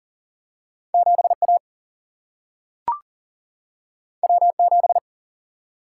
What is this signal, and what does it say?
Morse code, heard as a keyed tone, about 40 wpm, 700 Hz. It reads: 7A W7
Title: Single-Letter Single-Number 40wpm